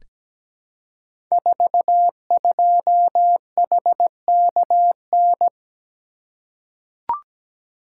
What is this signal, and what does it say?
Morse code, heard as a keyed tone, about 17 wpm, 700 Hz. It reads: 42HKN
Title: Code Groups - 5 Letter:Numbers - 17wpm